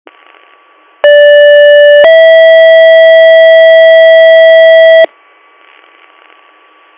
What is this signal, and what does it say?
Two-tone page, heard as a one-second tone, then a three-second tone, about 580 then 650 Hz, with noise. It is heard from 1.0 to 5.1 s.